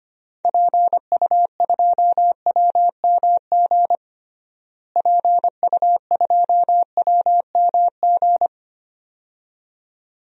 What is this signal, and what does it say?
Morse code, heard as a keyed tone, about 25 wpm, 700 Hz. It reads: PU2WMG PU2WMG
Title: DXCC-100 Call Signs - 6 Characters - Repeated 2x - 25wpm